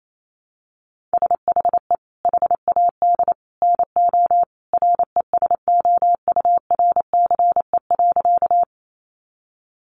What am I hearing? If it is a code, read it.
Morse code, heard as a keyed tone, about 28 wpm, 700 Hz. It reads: SHE HAD NO RESOURCE.
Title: Sentences from Top 1300 Words 28wpm